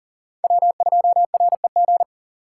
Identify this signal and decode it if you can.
Morse code, heard as a keyed tone, about 40 wpm, 700 Hz. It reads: W2REG
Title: USA Call Signs - 5 Characters - 2x Character Spacing - 40wpm